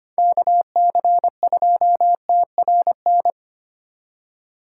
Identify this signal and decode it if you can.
Morse code, heard as a keyed tone, about 25 wpm, 700 Hz. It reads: KC2TRN
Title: USA Call Signs - 6 Characters - 2x Character Spacing - 25wpm